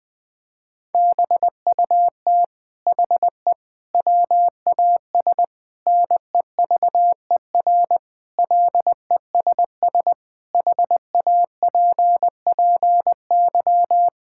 Morse code, heard as a keyed tone, about 21 wpm, 700 Hz. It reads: BUT HE WAS NEVER LESS HAPPY
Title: Sentences from Top 1000 Words - Review Part 5 - 20wpm